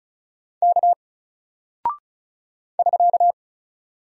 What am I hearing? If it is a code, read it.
Morse code, heard as a keyed tone, about 36 wpm, 700 Hz. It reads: K <SK>
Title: Single Character (Letter:Number:Punctuation:Prosign) Alt 35wpm